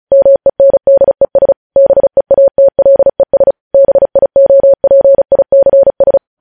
Morse code, heard as a keyed tone, about 36 wpm, 550 Hz. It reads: MENDES BEATLES BIOPICS
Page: News Headlines in Morse Code at 35 WPM